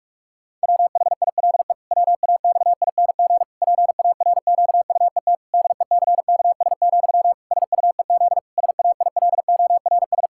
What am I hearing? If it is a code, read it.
Morse code, heard as a keyed tone, about 41 wpm, 700 Hz. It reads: WHILE WAXING PARQUET DECKS, SUEZ SAILORS
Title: Pangram Challenge v2 - 45wpm